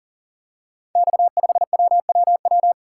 Morse code, heard as a keyed tone, about 42 wpm, 690 Hz. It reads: X5WWW
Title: Code Groups - 5 Letter:Numbers - 40wpm